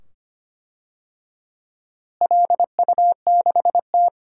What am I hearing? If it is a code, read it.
Morse code, heard as a keyed tone, about 25 wpm, 700 Hz. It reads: LU6T
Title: DXCC-100 Call Signs - 4 Characters - Repeated 2x - 25wpm